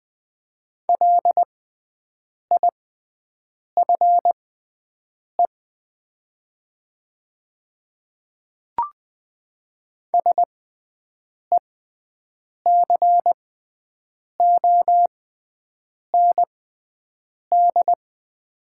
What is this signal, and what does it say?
Morse code, heard as a keyed tone, about 20 wpm, spaced as for 7 wpm, 700 Hz. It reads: LIFE SECOND
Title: 6x Character Spacing - Top 300 Words - 20wpm